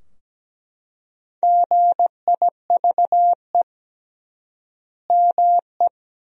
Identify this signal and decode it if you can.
Morse code, heard as a keyed tone, about 17 wpm, 700 Hz. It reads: GIVE ME
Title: 3x Word-Spacing - Sentences from Top 200 Words - 17wpm